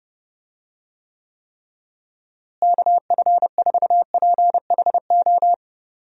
Morse code, heard as a keyed tone, about 32 wpm, 700 Hz. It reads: KF4PHO
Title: USA Call Signs - 6 Characters - Repeated 3x - 30wpm